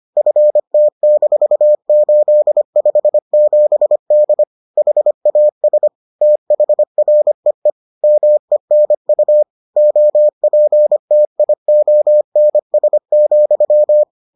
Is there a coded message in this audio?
Yes: Morse code from 0.2 to 14.0 s, heard as a keyed tone, about 25 wpm, 600 Hz. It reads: FT-857D HAS THREE MENU OPTIONS,